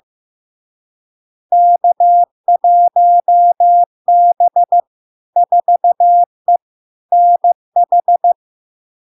Morse code, heard as a keyed tone, about 15 wpm, 700 Hz. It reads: K1B 4E NH
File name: LARK-FD-2022-Log-in-CW-15wpm-700-Hz